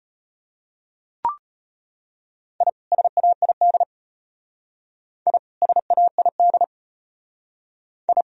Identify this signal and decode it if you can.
Morse code, heard as a keyed tone, about 30 wpm, 700 Hz. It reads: I SAID I SAID I
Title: Speed-Racing - Sets of 2 Words 25wpm